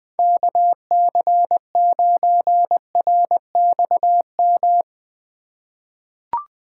Morse code, heard as a keyed tone, about 20 wpm, 700 Hz. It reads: KC9RXM
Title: USA Call Signs - 6 Characters - 2x Character Spacing - 20wpm